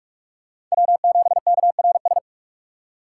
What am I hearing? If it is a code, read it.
Morse code, heard as a keyed tone, about 46 wpm, 700 Hz. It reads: W7KRS
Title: K1USN SST QSO Practice - 45wpm